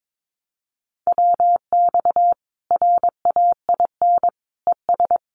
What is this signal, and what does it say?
Morse code, heard as a keyed tone, about 22 wpm, 700 Hz. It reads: WX RAIN ES
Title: Common QSO Phrases 22wpm